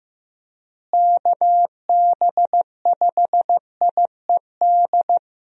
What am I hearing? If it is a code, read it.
Morse code, heard as a keyed tone, about 15 wpm, 700 Hz. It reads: KB5IED
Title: USA Call Signs - 6 Characters - Repeated 2x - 15wpm